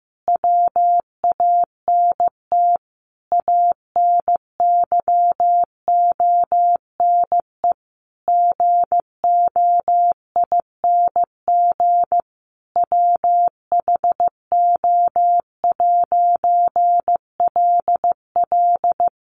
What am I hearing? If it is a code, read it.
Morse code, heard as a keyed tone, about 15 wpm, 700 Hz. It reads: WANT ANYONE GOING WHO'LL
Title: A Sound Of Thunder 15wpm